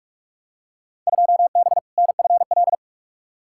Morse code, heard as a keyed tone, about 46 wpm, 700 Hz. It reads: 2B NFL
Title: Field Day Exchanges Encore - 45wpm